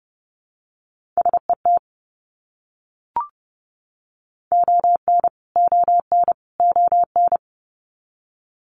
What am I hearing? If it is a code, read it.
Morse code, heard as a keyed tone, about 30 wpm, 700 Hz. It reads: SET ON ON ON
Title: Top 300 Words Repeated 30wpm